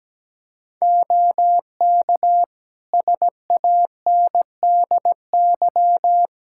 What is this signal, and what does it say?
Morse code, heard as a keyed tone, about 17 wpm, 700 Hz. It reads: OK SANDY
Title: K1USN SST QSO Practice 17wpm